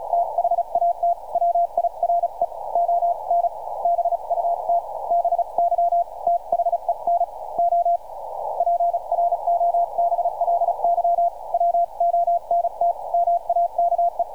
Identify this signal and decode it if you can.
Morse code, heard as a keyed tone, about 35 wpm, 690 Hz. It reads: THAT WIRE ON BUT BY THEN O PROBLY WONT MAKE
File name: archived / CRSnet